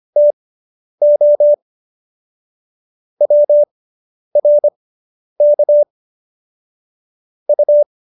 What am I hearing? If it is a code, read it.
Morse code, heard as a keyed tone, about 25 wpm, spaced as for 10 wpm, 590 Hz. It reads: TO WRK U